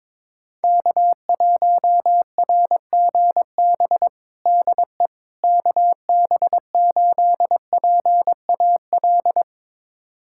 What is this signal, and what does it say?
Morse code, heard as a keyed tone, about 22 wpm, 700 Hz. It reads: K1RGB DE KB8PAL
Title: USA Callsign DE Callsign 22wpm